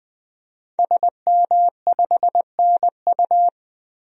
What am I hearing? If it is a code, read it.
Morse code, heard as a keyed tone, about 20 wpm, 700 Hz. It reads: SM5NU
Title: DXCC-100 Call Signs - 5 Characters - 20wpm